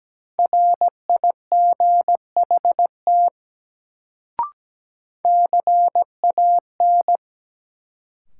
Morse code, heard as a keyed tone, about 17 wpm, 700 Hz. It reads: RIGHT CAN
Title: Sets of 2 Words - New Word Drill - 17wpm